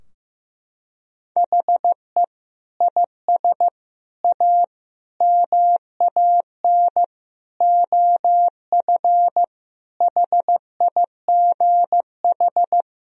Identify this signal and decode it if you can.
Morse code, heard as a keyed tone, about 15 wpm, 700 Hz. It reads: HE IS A MAN OF HIGH
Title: Sentences from Top 1000 Words - Review Part 2 - 15wpm